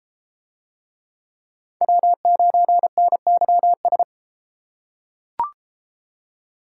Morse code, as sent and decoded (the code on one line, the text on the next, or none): .-- ----. -. -.-- ...
W9NYS